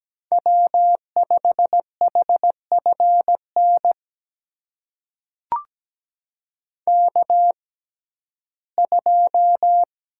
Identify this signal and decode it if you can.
Morse code, heard as a keyed tone, about 17 wpm, spaced as for 12 wpm, 700 Hz. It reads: W5HFN K2
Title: USA Call Signs - 5 Characters - 6x Character Spacing - 17wpm